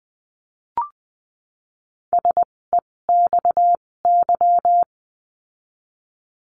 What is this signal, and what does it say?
Morse code, heard as a keyed tone, about 20 wpm, spaced as for 16 wpm, 700 Hz. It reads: SEXY
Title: 256 Lesson 42a - Words 12-20wpm